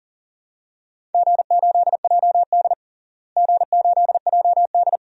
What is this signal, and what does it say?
Morse code, heard as a keyed tone, about 40 wpm, 700 Hz. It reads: G8JD G8JD
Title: DXCC-100 Call Signs - 4 Characters - Repeated 2x - 40wpm